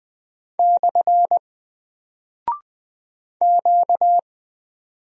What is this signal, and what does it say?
Morse code, heard as a keyed tone, about 20 wpm, 700 Hz. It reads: / Q
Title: Single Character (Letter:Number:Punctuation:Prosign) 20wpm